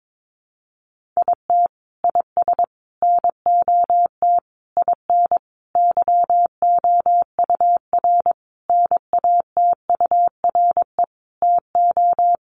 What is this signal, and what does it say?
Morse code, heard as a keyed tone, about 22 wpm, 700 Hz. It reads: IT IS NOT IN YOUR NATURE TO
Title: Sentences from Top 1200 Words 22wpm